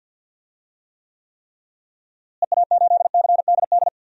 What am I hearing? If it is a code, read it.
Morse code, heard as a keyed tone, about 47 wpm, 700 Hz. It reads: EA8CDD
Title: DXCC-100 Call Signs - 6 Characters - Repeated 3x - 50wpm